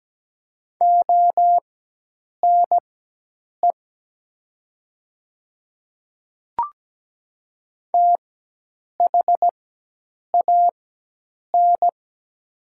Morse code, heard as a keyed tone, about 17 wpm, spaced as for 8 wpm, 700 Hz. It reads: ONE THAN
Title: 4x Character Spacing - Top 100 Words - 17wpm